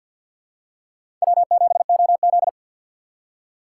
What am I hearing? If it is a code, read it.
Morse code, heard as a keyed tone, about 47 wpm, 700 Hz. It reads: W7OZ
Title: USA Call Signs - 4 Characters - 2x Character Spacing - 50wpm